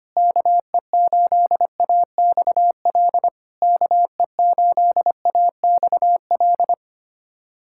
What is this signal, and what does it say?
Morse code, heard as a keyed tone, about 25 wpm, 690 Hz. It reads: KE8AXL KE8AXL
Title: SOTA QSO Practice 25wpm